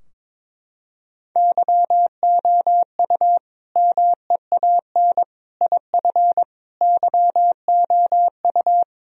Morse code, heard as a keyed tone, about 22 wpm, 700 Hz. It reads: YOU MEAN IF YOU
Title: Sentences from Top 1000 Words - Review Part 4 - 22wpm